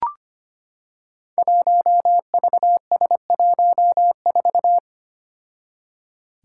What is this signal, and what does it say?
Morse code, heard as a keyed tone, about 25 wpm, 700 Hz. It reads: 1VS14
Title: Five Challenging Letter-Number Combinations 25wpm